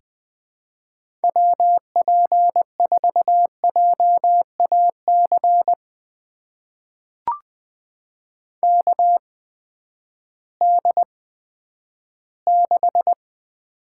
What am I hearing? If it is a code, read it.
Morse code, heard as a keyed tone, about 20 wpm, spaced as for 11 wpm, 700 Hz. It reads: WP4JAC KD6
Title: USA Call Signs - 6 Characters - 8x Character Spacing - 20wpm